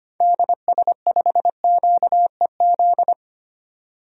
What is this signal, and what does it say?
Morse code, heard as a keyed tone, about 25 wpm, 700 Hz. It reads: DS5QEZ
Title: DXCC-100 Call Signs - 6 Characters - 25wpm